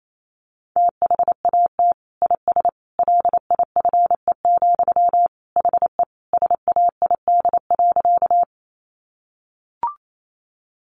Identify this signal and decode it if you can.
Morse code, heard as a keyed tone, about 28 wpm, 700 Hz. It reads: THAT IS LIFE, HE SAID.
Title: Sentences from Top 900 Words 28wpm